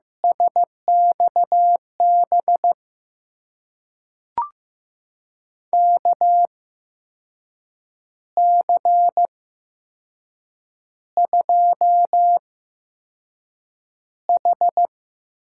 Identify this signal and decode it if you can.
Morse code, heard as a keyed tone, about 15 wpm, spaced as for 6 wpm, 700 Hz. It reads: SXB KC2H
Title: USA Call Signs - 6 Characters - 8x Character Spacing - 15wpm